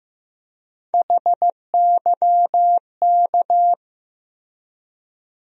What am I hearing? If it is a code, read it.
Morse code, heard as a keyed tone, about 15 wpm, 700 Hz. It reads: HYK
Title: USA Call Sign Postfixes - 3 Letters 15wpm